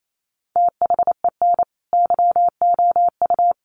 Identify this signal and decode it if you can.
Morse code, heard as a keyed tone, about 28 wpm, 700 Hz. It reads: THEN YOU
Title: Sentences from Top 400 Words 28wpm